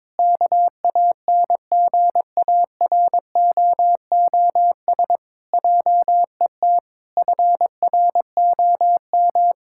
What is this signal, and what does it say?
Morse code, heard as a keyed tone, about 22 wpm, 700 Hz. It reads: KANGAROOS JET FROM
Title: Pangram Challenge v2 - 22wpm